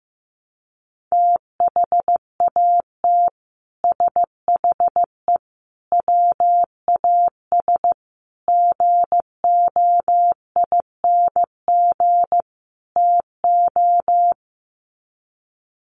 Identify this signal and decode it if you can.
Morse code, heard as a keyed tone, about 15 wpm, 700 Hz. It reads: THAT SHE WAS GOING TO
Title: Sets of 5 Words Encore2 15wpm